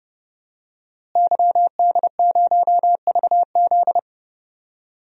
Morse code, heard as a keyed tone, about 32 wpm, 700 Hz. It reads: YD0VZ
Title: DXCC-100 Call Signs - 5 Characters - Repeated 2x - 30wpm